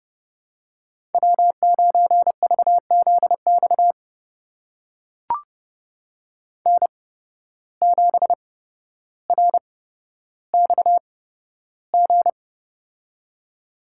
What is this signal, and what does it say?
Morse code, heard as a keyed tone, about 30 wpm, spaced as for 13 wpm, 700 Hz. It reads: W9VZX N7RXG
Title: USA Call Signs - 5 Characters - 8x Character Spacing - 30wpm